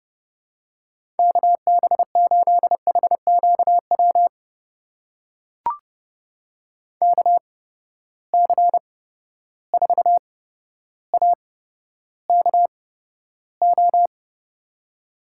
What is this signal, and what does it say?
Morse code, heard as a keyed tone, about 30 wpm, spaced as for 13 wpm, 700 Hz. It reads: KB8HQW KC4AKO
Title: USA Call Signs - 6 Characters - 8x Character Spacing - 30wpm